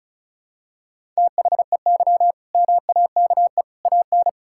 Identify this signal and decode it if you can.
Morse code, heard as a keyed tone, about 36 wpm, 700 Hz. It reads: THEY MAKE AN
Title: Sentences from Top 1000 Words - Review Part 6 - 35wpm